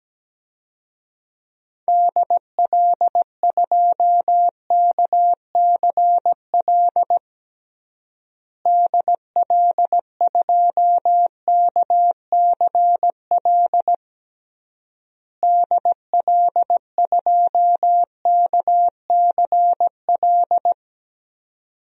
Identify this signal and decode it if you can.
Morse code, heard as a keyed tone, about 17 wpm, 700 Hz. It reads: DL2KCL DL2KCL DL2KCL
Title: DXCC-100 Call Signs - 6 Characters - Repeated 3x - 17wpm